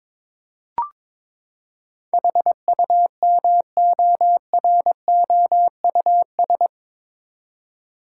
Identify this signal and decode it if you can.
Morse code, heard as a keyed tone, about 22 wpm, 700 Hz. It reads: HUMOROUS
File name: practice / 22wpm / Eight Letter Words - 22wpm